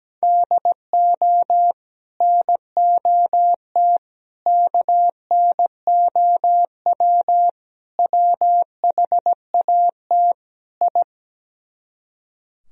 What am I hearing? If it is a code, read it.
Morse code, heard as a keyed tone, about 17 wpm, 700 Hz. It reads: DO NOT KNOW WHAT I
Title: Speed-Racing - Sets of 5 Words 17wpm